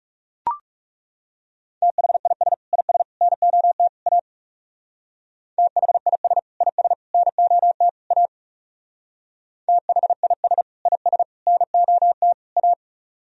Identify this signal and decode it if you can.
Morse code, heard as a keyed tone, about 37 wpm, 700 Hz. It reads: THIS IS NOT A THIS IS NOT A THIS IS NOT A
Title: Speed-Racing - Sets of 4 Words 30wpm